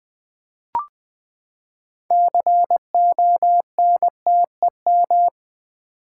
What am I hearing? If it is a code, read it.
Morse code, heard as a keyed tone, about 20 wpm, 700 Hz. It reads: CONTEM
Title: Build A Word - 6 Letters At A Time - 12-Letter Words - 20wpm